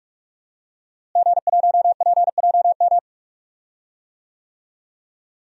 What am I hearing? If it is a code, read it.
Morse code, heard as a keyed tone, about 43 wpm, 700 Hz. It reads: G1PJM
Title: DXCC-100 Call Signs - 5 Characters - Repeated 3x - 45wpm